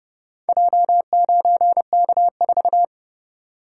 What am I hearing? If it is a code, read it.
Morse code, heard as a keyed tone, about 29 wpm, 700 Hz. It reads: J9K4